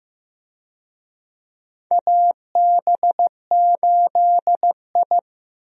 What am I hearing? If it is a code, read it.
Morse code, heard as a keyed tone, about 15 wpm, 700 Hz. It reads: AB8I
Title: USA Call Signs - 4 Characters - Repeated 2x - 15wpm